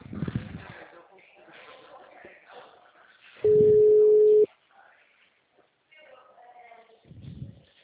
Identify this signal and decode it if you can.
Ringback tone (Europe, 425 Hz), with speech.